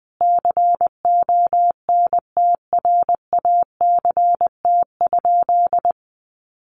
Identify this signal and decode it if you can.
Morse code, heard as a keyed tone, about 20 wpm, 700 Hz. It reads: CONTRACT?
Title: Sentences from Top 1400 Words 20wpm